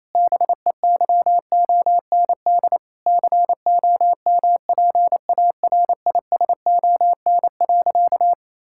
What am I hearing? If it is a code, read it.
Morse code, heard as a keyed tone, about 28 wpm, 700 Hz. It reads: BEYOND COMPARISON.
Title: Sentences from Top 1600 Words 28wpm